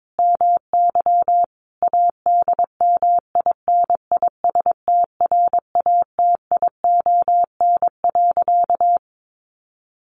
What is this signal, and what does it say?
Morse code, heard as a keyed tone, about 22 wpm, 700 Hz. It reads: MY ADMINISTRATION.
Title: Sentences from Top 1300 Words 22wpm